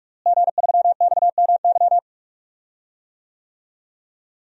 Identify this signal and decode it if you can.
Morse code, heard as a keyed tone, about 41 wpm, 700 Hz. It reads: G3XMY
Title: DXCC-100 Call Signs - 5 Characters - Repeated 2x - 45wpm